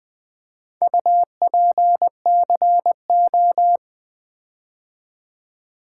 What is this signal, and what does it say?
Morse code, heard as a keyed tone, about 20 wpm, 700 Hz. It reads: UPCO
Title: Build A Word - 4 Letters At A Time - 8-Letter Words - 20wpm